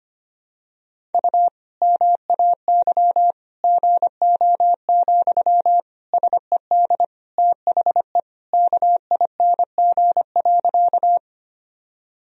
Morse code, heard as a keyed tone, about 25 wpm, 700 Hz. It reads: U MAY GO, SED THE KING.
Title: Sentences from Top 400 Words and Abbreviations 25wpm